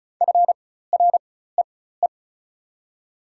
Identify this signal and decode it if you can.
Morse code, heard as a keyed tone, about 34 wpm, spaced as for 16 wpm, 690 Hz. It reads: FREE
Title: 4x Character Spacing - Top 500 Words - 35wpm